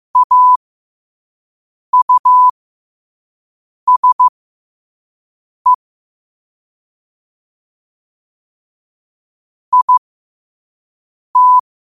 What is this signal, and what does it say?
Morse code, heard as a keyed tone, about 15 wpm, spaced as for 5 wpm, 1000 Hz. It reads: AUSE IT
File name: qotdfiles / 5wpm